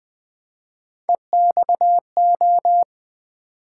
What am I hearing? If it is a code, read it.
Morse code, heard as a keyed tone, about 20 wpm, 700 Hz. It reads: EXO
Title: Build A Word - 3 Letters At A Time - 6-Letter Words - 20wpm